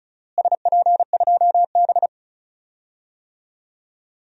Morse code, heard as a keyed tone, about 34 wpm, 700 Hz. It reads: SP2B
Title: DXCC-100 Call Signs - 4 Characters - Repeated 2x - 35wpm